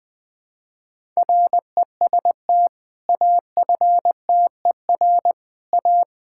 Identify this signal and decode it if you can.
Morse code, heard as a keyed tone, about 20 wpm, 700 Hz. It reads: REST AFTER A
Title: Sentences with 2-Word Phrases - Spoken Beginning - 20wpm